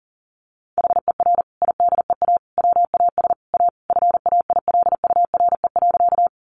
Morse code, heard as a keyed tone, about 40 wpm, 700 Hz. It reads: HER IDEA WAS A FAILURE.
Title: Sentences from Top 900 Words 40wpm